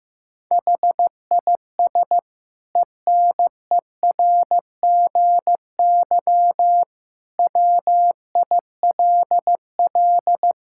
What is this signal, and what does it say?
Morse code, heard as a keyed tone, about 15 wpm, 700 Hz. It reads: HIS ENERGY WILL
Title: Sentences from Top 1000 Words - Review Part 6 - 15wpm